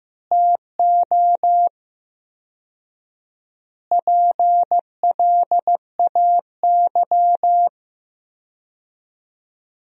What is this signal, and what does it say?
Morse code, heard as a keyed tone, about 15 wpm, 700 Hz. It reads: TO PLAY
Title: 4x Word-Spacing - Sentences from Top 300 Words - 15wpm